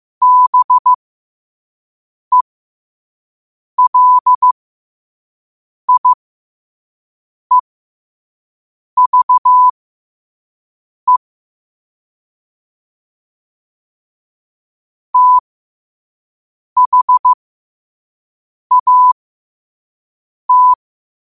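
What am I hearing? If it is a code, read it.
Morse code, heard as a keyed tone, about 15 wpm, spaced as for 5 wpm, 1000 Hz. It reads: BELIEVE THAT
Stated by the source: Morse practice quote sent at 5 wpm.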